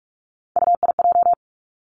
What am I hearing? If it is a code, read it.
Morse code, heard as a keyed tone, about 46 wpm, 700 Hz. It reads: UIJ
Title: Three Challenging Letter-Number Combinations 45wpm